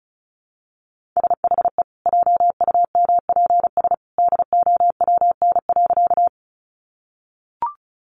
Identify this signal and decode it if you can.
Morse code, heard as a keyed tone, about 34 wpm, 700 Hz. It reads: SHE JUMPS DOWN.
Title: Easy Phrases 35wpm